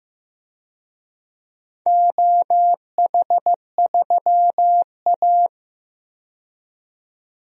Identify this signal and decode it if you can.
Morse code, heard as a keyed tone, about 15 wpm, 700 Hz. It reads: OH3A
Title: DXCC-100 Call Signs - 4 Characters - 15wpm